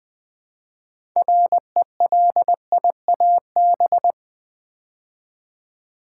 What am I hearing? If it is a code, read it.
Morse code, heard as a keyed tone, about 20 wpm, 700 Hz. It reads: RELIAB
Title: Build A Word - 2 Letters At A Time - 8-Letter Words - 20wpm